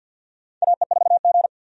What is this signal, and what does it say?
Morse code, heard as a keyed tone, about 53 wpm, 700 Hz. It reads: AE4G
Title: USA Call Signs - 4 Characters - 50wpm